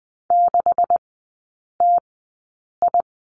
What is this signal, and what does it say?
Morse code, heard as a keyed tone, about 20 wpm, spaced as for 8 wpm, 700 Hz. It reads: 6TI
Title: 127 Lesson 21e - Number 6 - Three Characters 20wpm